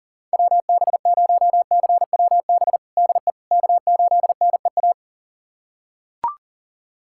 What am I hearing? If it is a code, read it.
Morse code, heard as a keyed tone, about 40 wpm, 700 Hz. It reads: WB0CWB DE K8NEA
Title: USA Callsign DE Callsign 40wpm